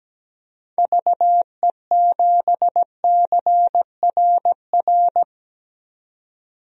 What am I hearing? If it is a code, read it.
Morse code, heard as a keyed tone, about 17 wpm, 700 Hz. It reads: VE7CRR
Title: DXCC-100 Call Signs - 6 Characters - 17wpm